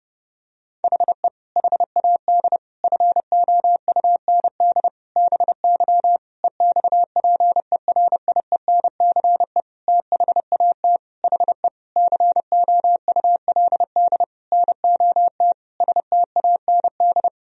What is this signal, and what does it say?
Morse code, heard as a keyed tone, about 32 wpm, 700 Hz. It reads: HE HAD FOUND BY EXPERIENCE THAT HE COULD NOT STAND
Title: Sentences from Top 1000 Words - Review Part 3 - 30wpm